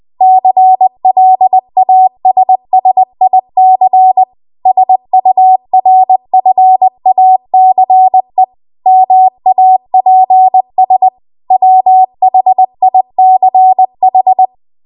Morse code, heard as a keyed tone, about 20 wpm, 750 Hz. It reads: CLASSIC SURFACE MAPS WHICH
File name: Archive / 20 WPM / 20WPM